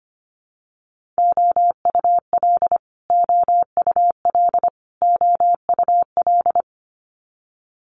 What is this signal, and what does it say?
Morse code, heard as a keyed tone, about 25 wpm, 700 Hz. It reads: OUL OUL OUL
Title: Multi-letter Phonemes Repeated 3x 25wpm